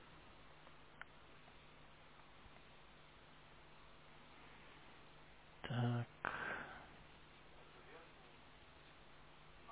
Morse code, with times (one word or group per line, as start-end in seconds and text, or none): none